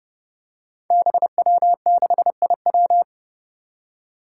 Morse code, heard as a keyed tone, about 30 wpm, 700 Hz. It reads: BW6IW
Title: Five Challenging Letter-Number Combinations 30wpm